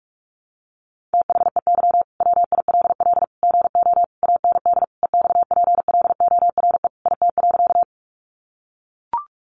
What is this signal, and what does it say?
Morse code, heard as a keyed tone, about 44 wpm, 700 Hz. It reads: THEY WILL GO AND EXPLORE IT.
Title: Sentences from Top 1100 Words 45wpm